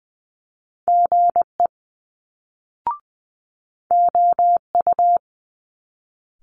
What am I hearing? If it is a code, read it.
Morse code, heard as a keyed tone, about 20 wpm, 700 Hz. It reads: GE OU
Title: Multi-letter Phonemes Quick Fire 20wpm